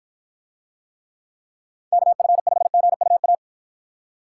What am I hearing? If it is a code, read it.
Morse code, heard as a keyed tone, about 49 wpm, 700 Hz. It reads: KF5GUA